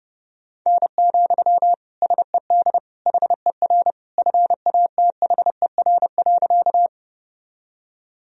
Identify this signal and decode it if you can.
Morse code, heard as a keyed tone, about 32 wpm, 700 Hz. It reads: N, SED HER FATHER.
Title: Sentences from Top 300 Words and Abbreviations 30wpm